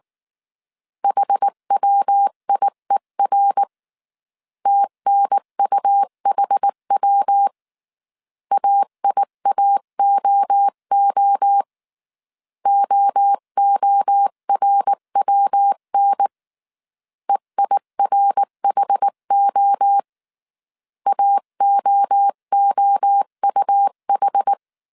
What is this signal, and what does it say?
Morse code, heard as a keyed tone, about 19 wpm, 800 Hz. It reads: HWIER TNUHW AIAOO OORWN EIRHO AOOUH
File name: CW / DET / 1 3 1500C S B 18wpm 1916